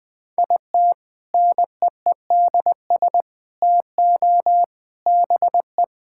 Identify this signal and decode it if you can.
Morse code, heard as a keyed tone, about 21 wpm, 700 Hz. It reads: IT NEEDS TO BE
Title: Sentences with 3-Word Phrases - 20wpm